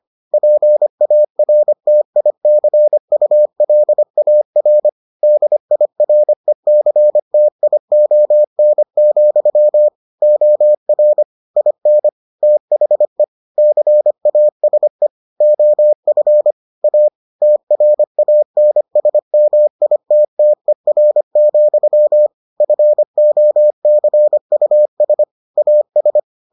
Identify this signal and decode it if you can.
Morse code, heard as a keyed tone, about 25 wpm, 600 Hz. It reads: PARTICULAR DIRECTION, OR IN THE CASE OF A TRANSMITTER, FOCUS AS